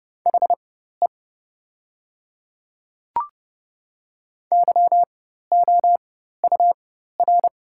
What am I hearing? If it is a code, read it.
Morse code, heard as a keyed tone, about 30 wpm, spaced as for 14 wpm, 700 Hz. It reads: HE YOUR
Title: 4x Character Spacing - Top 100 Words - 30wpm